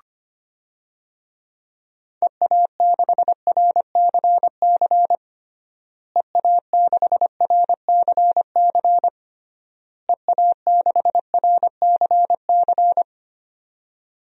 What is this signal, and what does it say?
Morse code, heard as a keyed tone, about 25 wpm, 700 Hz. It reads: EA6RCC EA6RCC EA6RCC
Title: DXCC-100 Call Signs - 6 Characters - Repeated 3x - 25wpm